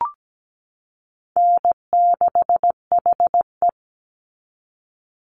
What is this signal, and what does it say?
Morse code, heard as a keyed tone, about 17 wpm, 700 Hz. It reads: N6HE
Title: Easy Call Signs Repeated 2x - 17wpm